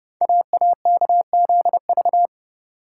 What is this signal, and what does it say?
Morse code, heard as a keyed tone, about 30 wpm, 700 Hz. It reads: AAKZV
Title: Code Groups - 5 Letter:Numbers - 30wpm